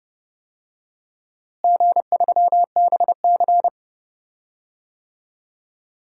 Morse code, heard as a keyed tone, about 29 wpm, 700 Hz. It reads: G3BC
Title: DXCC-100 Call Signs - 4 Characters - Repeated 3x - 30wpm